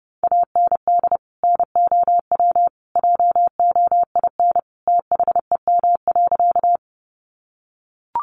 Morse code, heard as a keyed tone, about 30 wpm, 700 Hz. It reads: AND NOW JOIN THEM.
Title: Sentences from Top 1400 Words 30wpm